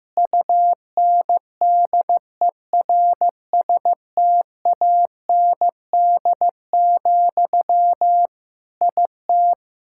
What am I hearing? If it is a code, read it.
Morse code, heard as a keyed tone, about 15 wpm, 700 Hz. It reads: UNDERSTAND, IT
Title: Sentences from Top 1000 Words - Review Part 5 - 15wpm